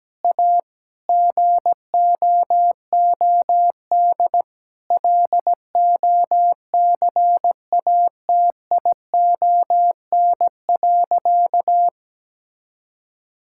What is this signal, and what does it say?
Morse code, heard as a keyed tone, about 17 wpm, 700 Hz. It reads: A GOOD LOCATION.
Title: Sentences from Top 1000 Words - Review Part 6 - 17wpm